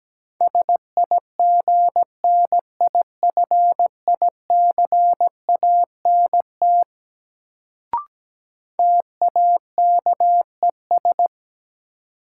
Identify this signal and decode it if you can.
Morse code, heard as a keyed tone, about 17 wpm, 700 Hz. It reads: SIGNIFICANT TAKES
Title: Sentences from Top 1000 Words - Review Part 6 - New Word Drill - 17wpm